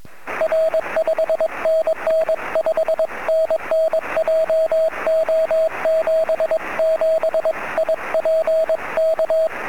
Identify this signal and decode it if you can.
Morse code, heard as a keyed tone, about 22 wpm, 640 Hz, with radio noise. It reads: R 5NN 5NN JO77IP K
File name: sm7lcb / 09-sm7gep 23cm